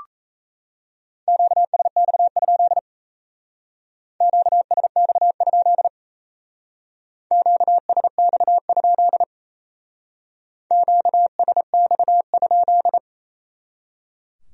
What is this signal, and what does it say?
Morse code, heard as a keyed tone, about 35 wpm, 700 Hz. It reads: QSX? QSX? QSX? QSX?